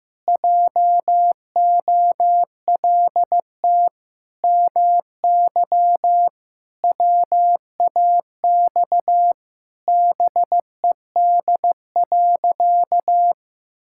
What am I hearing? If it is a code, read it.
Morse code, heard as a keyed tone, about 15 wpm, 700 Hz. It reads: JOLT MY WAX BED.
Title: Pangram Challenge v2 - 15wpm